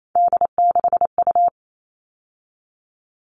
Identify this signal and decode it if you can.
Morse code, heard as a keyed tone, about 28 wpm, 700 Hz. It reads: D6U
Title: All Cut Numbers - 3-Digit 28wpm